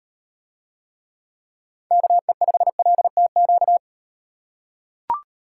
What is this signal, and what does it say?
Morse code, heard as a keyed tone, about 38 wpm, 700 Hz. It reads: KE5LTQ